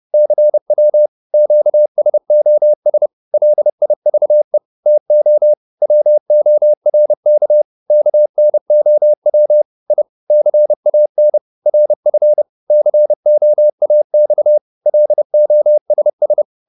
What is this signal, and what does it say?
Morse code, heard as a keyed tone, about 30 wpm, 600 Hz. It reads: CW QSOS LIVE TO WORK KNOW I CAN RF COAX LOSS